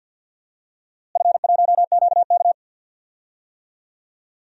Morse code, heard as a keyed tone, about 48 wpm, 700 Hz. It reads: F1QK
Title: DXCC-100 Call Signs - 4 Characters - Repeated 2x - 50wpm